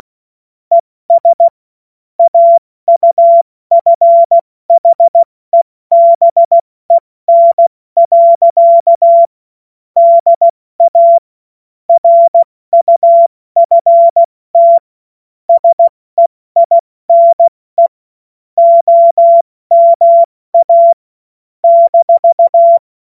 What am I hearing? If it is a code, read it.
Morse code, heard as a keyed tone, about 16 wpm, 680 Hz. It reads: ES AUFHEBEN. DA RUFT SEINE OMA -
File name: Morsekurs / Witze / Witz 17. August 2021, 70 F80